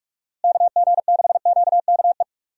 Morse code, heard as a keyed tone, about 46 wpm, 700 Hz. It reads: KG6QKE
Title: USA Call Signs - 6 Characters - Repeated 2x - 45wpm